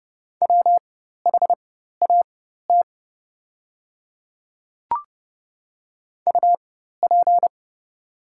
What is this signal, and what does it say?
Morse code, heard as a keyed tone, about 30 wpm, spaced as for 14 wpm, 700 Hz. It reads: WHAT UP
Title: 4x Character Spacing - Top 100 Words - 30wpm